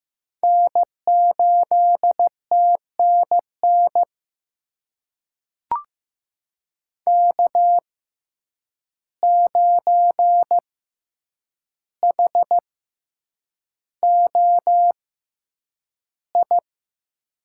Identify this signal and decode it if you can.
Morse code, heard as a keyed tone, about 15 wpm, spaced as for 8 wpm, 700 Hz. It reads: N8TNN K9HOI
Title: USA Call Signs - 5 Characters - 6x Character Spacing - 15wpm